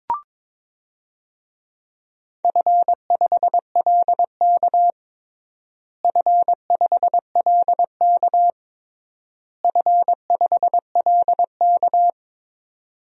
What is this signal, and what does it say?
Morse code, heard as a keyed tone, about 22 wpm, 700 Hz. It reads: F5LK F5LK F5LK
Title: DXCC-100 Call Signs - 4 Characters - Repeated 3x - 22wpm